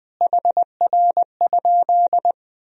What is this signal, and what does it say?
Morse code, heard as a keyed tone, about 20 wpm, 700 Hz. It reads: HR?
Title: Sentences from Top 300 Words and Abbreviations 20wpm